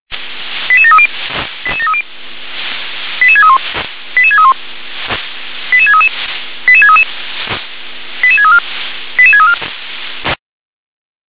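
Five-tone selective call (ZVEI), heard as quick runs of five tones, tones of about 70 ms. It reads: 88730 88730 88631 88631 88633 88633 88634 88634
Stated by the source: ZVEI five-tone selective call received on SDR.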